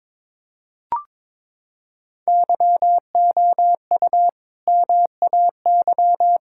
Morse code, heard as a keyed tone, about 22 wpm, 700 Hz. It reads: YOU MAY
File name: practice / 22wpm / Sets of 8 Words - 22wpm